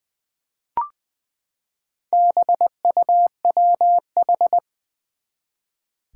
Morse code, heard as a keyed tone, about 20 wpm, 700 Hz. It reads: BUWH
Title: Four Challenging Letter-Number Combinations 20wpm